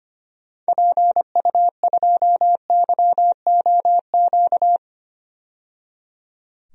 Morse code, heard as a keyed tone, about 25 wpm, 690 Hz. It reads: PU2YOQ
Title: DXCC-100 Call Signs - 6 Characters - 25wpm